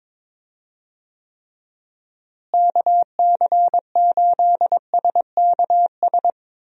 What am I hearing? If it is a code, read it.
Morse code, heard as a keyed tone, about 22 wpm, 700 Hz. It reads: KC8SKS
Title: USA Call Signs - 6 Characters - Repeated 3x - 22wpm